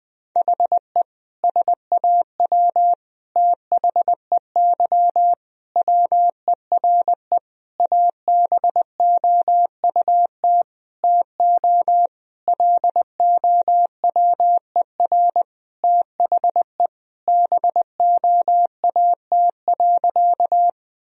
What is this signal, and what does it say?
Morse code, heard as a keyed tone, about 20 wpm, 700 Hz. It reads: HE SAW THEY WERE ABOUT TO LOWER THE BOAT.
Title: Sentences from Top 1000 Words - Review Part 4 - 20wpm